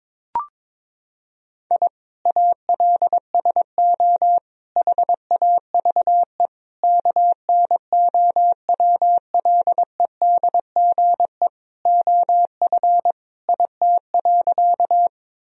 Morse code, heard as a keyed tone, about 22 wpm, 700 Hz. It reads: I ALSO HAVE KNOWLEDGE OF IT.
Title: Sentences from Top 1000 Words - Review Part 3 - 22wpm